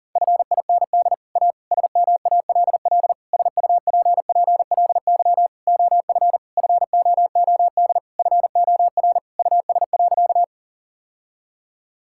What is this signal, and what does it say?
Morse code, heard as a keyed tone, about 40 wpm, 700 Hz. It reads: FIND A SMALL SUPPLY OF FOOD FOR US.
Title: Sentences from Top 1000 Words - Review Part 5 - 40wpm